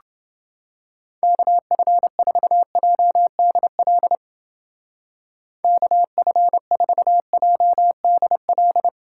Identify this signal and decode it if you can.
Morse code, heard as a keyed tone, about 30 wpm, 700 Hz. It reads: KF4JDL KF4JDL